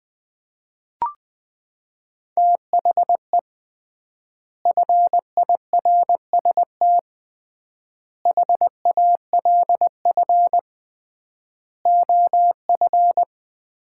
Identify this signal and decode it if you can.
Morse code, heard as a keyed tone, about 19 wpm, 700 Hz. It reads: THE FIRST HALF OF
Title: 3x Word-Spacing - Sets of 5 Words - 20wpm